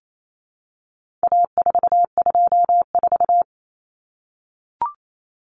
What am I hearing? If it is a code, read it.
Morse code, heard as a keyed tone, about 28 wpm, 700 Hz. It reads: A424
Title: Common Cut Numbers - 4-Digit 28wpm